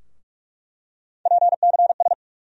Morse code, heard as a keyed tone, about 46 wpm, 700 Hz. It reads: PCS
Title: USA Call Sign Postfixes - 3 Letters 45wpm